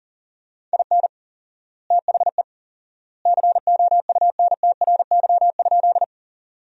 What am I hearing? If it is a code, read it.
Morse code, heard as a keyed tone, about 40 wpm, 700 Hz. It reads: IN THE COUNTRY?
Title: 4x Word-Spacing - Sentences from Top 300 Words - 40wpm